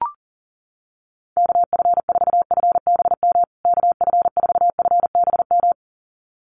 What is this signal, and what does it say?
Morse code, heard as a keyed tone, about 40 wpm, 700 Hz. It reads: KF4FBM KF4FBM
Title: USA Call Signs Repeated 2x 40wpm